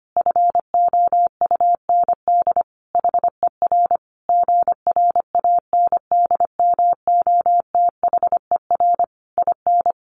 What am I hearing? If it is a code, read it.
Morse code, heard as a keyed tone, about 25 wpm, 700 Hz. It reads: FOUND HER GRANDMOTHER IN
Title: Sentences from Top 1200 Words 25wpm